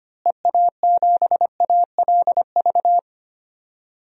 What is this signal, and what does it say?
Morse code, heard as a keyed tone, about 25 wpm, 700 Hz. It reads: EA7ALV
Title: DXCC-100 Call Signs - 6 Characters - Repeated 3x - 25wpm